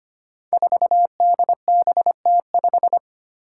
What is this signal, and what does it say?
Morse code, heard as a keyed tone, about 25 wpm, 700 Hz. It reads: 4DBT5
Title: Five Challenging Letter-Number Combinations 25wpm